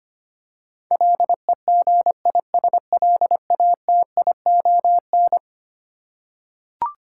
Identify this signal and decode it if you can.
Morse code, heard as a keyed tone, about 25 wpm, 700 Hz. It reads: LEGISLATION
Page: morse-code-ninja-practice-25wpm